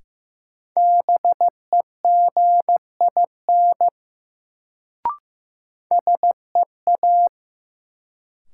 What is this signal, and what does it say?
Morse code, heard as a keyed tone, about 15 wpm, 700 Hz. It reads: BEGIN SEA
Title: Sentences from Top 300 Words - New Word Drill - 15wpm